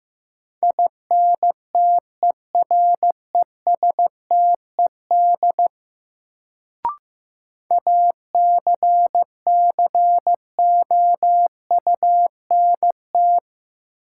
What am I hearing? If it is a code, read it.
Morse code, heard as a keyed tone, about 15 wpm, 700 Hz. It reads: INTERESTED ACCOUNT
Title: Sentences from Top 900 Words - New Word Drill - 15wpm